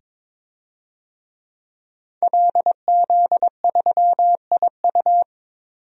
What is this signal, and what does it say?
Morse code, heard as a keyed tone, about 22 wpm, 700 Hz. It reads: LZ3IU
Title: DXCC-100 Call Signs - 5 Characters - 22wpm